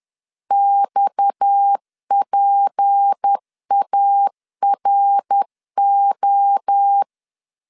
Morse code, heard as a keyed tone, about 10 wpm, 800 Hz. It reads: XPARO
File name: LEZIONI / 6 PKX0 / 6 PKX 10wpm 2426